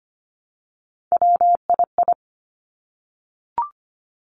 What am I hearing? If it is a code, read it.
Morse code, heard as a keyed tone, about 25 wpm, 700 Hz. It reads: WII
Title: Three Challenging Letter-Number Combinations 25wpm